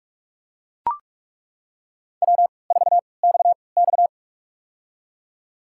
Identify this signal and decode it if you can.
Morse code, heard as a keyed tone, about 42 wpm, spaced as for 25 wpm, 700 Hz. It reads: W4XX
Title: USA Call Signs - 4 Characters - 3x Character Spacing - 45wpm